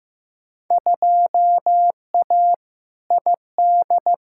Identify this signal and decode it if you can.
Morse code, heard as a keyed tone, about 15 wpm, 700 Hz. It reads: 2A ID
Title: Field Day Exchanges Encore - 15wpm